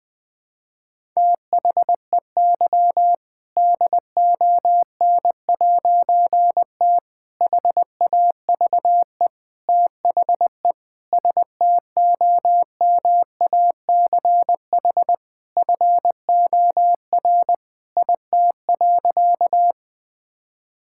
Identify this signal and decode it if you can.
Morse code, heard as a keyed tone, about 21 wpm, 700 Hz. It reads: THEY DON'T HAVE THE STOMACH FOR IT.
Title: All-Age Appropriate Jokes - 20wpm